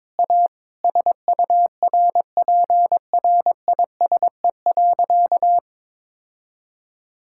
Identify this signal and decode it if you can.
Morse code, heard as a keyed tone, about 22 wpm, 700 Hz. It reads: A SURPRISE.
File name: practice / 22wpm / Sentences with 4-Word Phrases - 22wpm